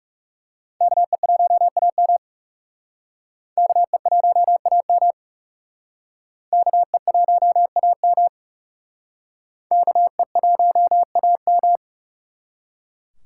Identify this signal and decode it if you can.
Morse code, heard as a keyed tone, about 37 wpm, 700 Hz. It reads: KE1AM KE1AM KE1AM KE1AM
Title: Speed Racing - USA 2x2 Call Signs 30wpm